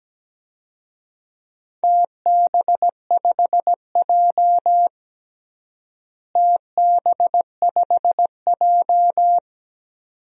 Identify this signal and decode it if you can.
Morse code, heard as a keyed tone, about 17 wpm, 700 Hz. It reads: TB5J TB5J
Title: DXCC-100 Call Signs - 4 Characters - Repeated 2x - 17wpm